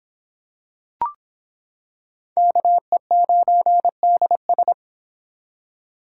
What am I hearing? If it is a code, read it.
Morse code, heard as a keyed tone, about 25 wpm, 700 Hz. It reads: KE9DS